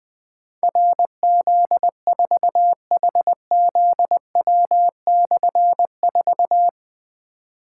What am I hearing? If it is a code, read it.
Morse code, heard as a keyed tone, about 20 wpm, 700 Hz. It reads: RZ4HZW/4
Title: Difficult Call Signs 20wpm